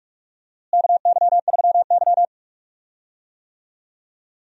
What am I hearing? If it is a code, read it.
Morse code, heard as a keyed tone, about 45 wpm, 700 Hz. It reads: KY3Y
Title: USA Call Signs - 4 Characters - Repeated 2x - 45wpm